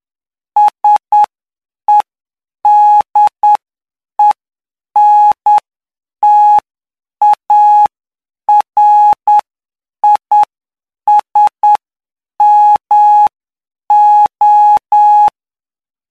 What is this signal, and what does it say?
Morse code, heard as a keyed tone, about 9 wpm, spaced as for 7 wpm, 840 Hz. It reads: SEDENTARISMO
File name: cw / py2lcd / 07-PY2LCD-Li��o 16